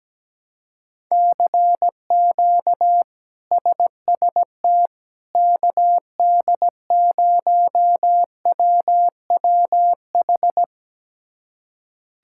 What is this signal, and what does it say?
Morse code, heard as a keyed tone, about 17 wpm, 700 Hz. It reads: CQ SST KD0WWH
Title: K1USN SST QSO Practice - 17wpm